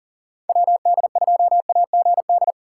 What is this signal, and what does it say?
Morse code, heard as a keyed tone, about 49 wpm, 700 Hz. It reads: WD2AGD
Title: Speed Racing - USA Call Signs 40wpm